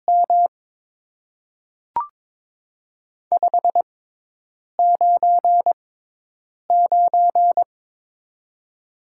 Morse code, heard as a keyed tone, about 22 wpm, spaced as for 8 wpm, 700 Hz. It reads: M 599
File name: Top 90 QSO Elements - 6x Character Spacing - 22wpm